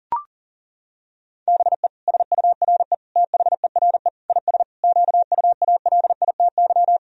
Morse code, heard as a keyed tone, about 40 wpm, 700 Hz. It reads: BE SURE THERE IS QUALITY
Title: Sentences from Top 1000 Words - Review Part 4 - 40wpm